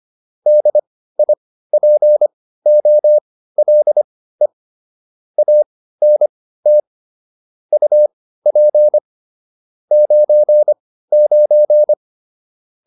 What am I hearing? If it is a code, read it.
Morse code, heard as a keyed tone, about 25 wpm, spaced as for 15 wpm, 600 Hz. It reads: DIPOLE ANT UP 99